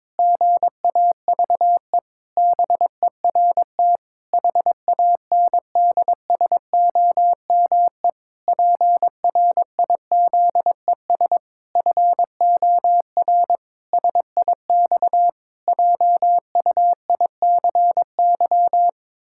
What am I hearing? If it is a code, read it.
Morse code, heard as a keyed tone, about 22 wpm, 700 Hz. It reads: GAVE BERT HANDSOME PRIZES FOR SIX JUICY
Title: Pangram Challenge v2 - 22wpm